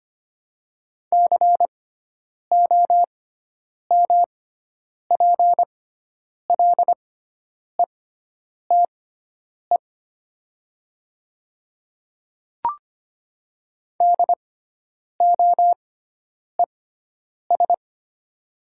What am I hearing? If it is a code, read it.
Morse code, heard as a keyed tone, about 25 wpm, spaced as for 9 wpm, 700 Hz. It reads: COMPLETE DOES
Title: 6x Character Spacing - Top 500 Words - 25wpm